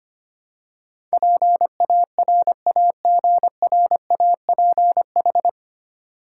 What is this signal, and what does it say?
Morse code, heard as a keyed tone, about 25 wpm, 700 Hz. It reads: PARAGRAPH
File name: Nine Letter Words - 25wpm